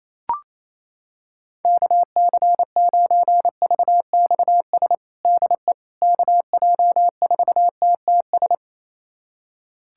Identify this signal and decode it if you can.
Morse code, heard as a keyed tone, about 28 wpm, 700 Hz. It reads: KC9VXS DE KJ4TTS
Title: USA Callsign DE Callsign - 6 Characters - 28wpm